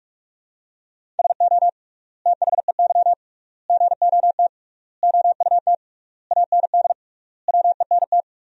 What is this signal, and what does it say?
Morse code, heard as a keyed tone, about 46 wpm, 700 Hz. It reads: SO THEY GOT OUT AND WENT
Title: 3x Word-Spacing - Sentences from Top 300 Words - 45wpm